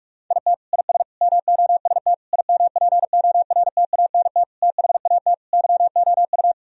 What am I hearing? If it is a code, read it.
Morse code, heard as a keyed tone, about 46 wpm, 700 Hz. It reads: IT IS MOST IMPORTANT THAT YOU
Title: Sentences from Top 1000 Words - Review Part 1 - 45wpm